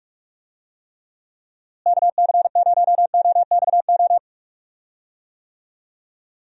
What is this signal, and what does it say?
Morse code, heard as a keyed tone, about 46 wpm, 700 Hz. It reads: KC0OXO
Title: USA Call Signs - 6 Characters - Repeated 2x - 45wpm